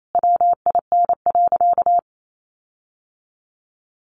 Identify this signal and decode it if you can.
Morse code, heard as a keyed tone, about 28 wpm, 700 Hz. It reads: WIN.
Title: Sentences from Top 1500 Words 28wpm